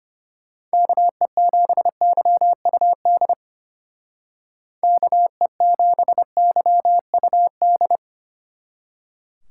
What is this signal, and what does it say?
Morse code, heard as a keyed tone, about 30 wpm, 700 Hz. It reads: KE7YUD KE7YUD
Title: Speed Racing - USA 2x3 Call Signs 25wpm